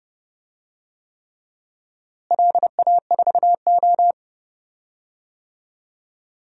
Morse code, heard as a keyed tone, about 30 wpm, 700 Hz. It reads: LA4O
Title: DXCC-100 Call Signs - 4 Characters - Repeated 3x - 30wpm